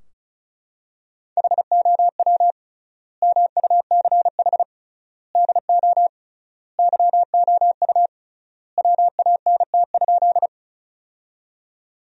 Morse code, heard as a keyed tone, about 35 wpm, 700 Hz. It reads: HOW MUCH DO YOU WANT?
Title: 3x Word-Spacing - Sentences from Top 200 Words - 35wpm